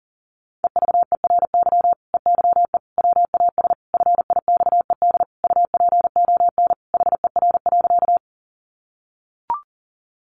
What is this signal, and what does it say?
Morse code, heard as a keyed tone, about 40 wpm, 700 Hz. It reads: EVERY EYE WAS FIXED UPON HER.
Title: Sentences from Top 1100 Words 40wpm